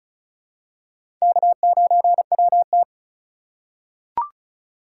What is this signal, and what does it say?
Morse code, heard as a keyed tone, about 36 wpm, 700 Hz. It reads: K9WT
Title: USA Call Signs - 4 Characters - 2x Character Spacing - 35wpm